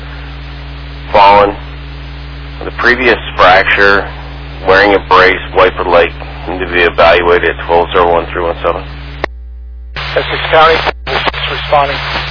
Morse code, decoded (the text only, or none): none